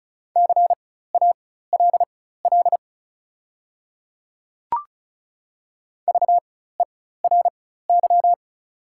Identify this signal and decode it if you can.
Morse code, heard as a keyed tone, about 36 wpm, spaced as for 16 wpm, 690 Hz. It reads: CALL VERY
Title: 4x Character Spacing - Top 200 Words - 35wpm